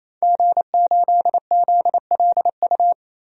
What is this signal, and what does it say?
Morse code, heard as a keyed tone, about 28 wpm, 700 Hz. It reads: G8ZLU
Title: DXCC-100 Call Signs - 5 Characters - 28wpm